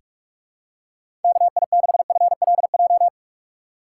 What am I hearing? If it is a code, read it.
Morse code, heard as a keyed tone, about 46 wpm, 700 Hz. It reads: KI6FLJ
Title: USA Call Signs - 6x Character Spacing - 45wpm